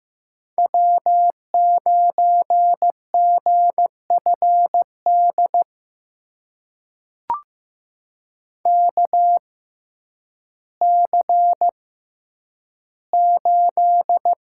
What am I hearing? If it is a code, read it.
Morse code, heard as a keyed tone, about 15 wpm, spaced as for 9 wpm, 700 Hz. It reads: W9GFD KC8
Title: USA Call Signs - 5 Characters - 6x Character Spacing - 15wpm